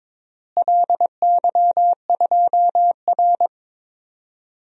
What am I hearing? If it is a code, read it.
Morse code, heard as a keyed tone, about 22 wpm, 700 Hz. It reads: LY2R
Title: DXCC-100 Call Signs - 4 Characters - Repeated 2x - 22wpm